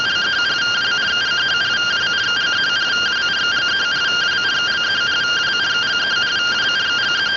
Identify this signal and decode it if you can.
Two-tone teleprinter signal, about 170 Hz shift.
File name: findings / MR CORONAL 55.gen-11 rtty rx